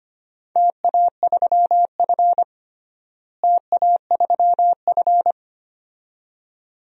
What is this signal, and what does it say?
Morse code, heard as a keyed tone, about 25 wpm, 700 Hz. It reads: TA3F TA3F
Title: DXCC-100 Call Signs - 4 Characters - Repeated 2x - 25wpm